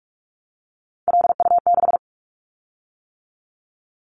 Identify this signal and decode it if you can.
Morse code, heard as a keyed tone, about 44 wpm, 700 Hz. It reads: LU6
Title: Callsign Prefixes for Top 100 DXCC - 45wpm